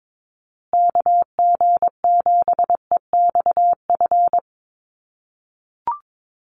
Morse code, heard as a keyed tone, about 22 wpm, 700 Hz. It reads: KG7EXF
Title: USA Call Signs Repeated 3x 22wpm